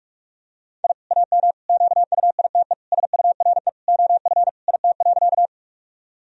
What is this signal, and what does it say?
Morse code, heard as a keyed tone, about 46 wpm, 700 Hz. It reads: I AM QUITE SURE OF IT.
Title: Sentences from Top 1000 Words - Review Part 2 - 45wpm